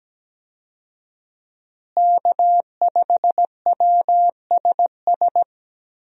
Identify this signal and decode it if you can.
Morse code, heard as a keyed tone, about 17 wpm, 700 Hz. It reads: K5WSS
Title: USA Call Signs - 5 Characters - Repeated 2x - 17wpm